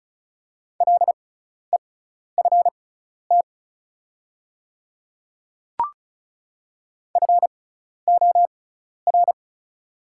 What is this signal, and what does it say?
Morse code, heard as a keyed tone, about 36 wpm, spaced as for 12 wpm, 700 Hz. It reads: LEFT FOR
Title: 6x Character Spacing - Top 500 Words - 35wpm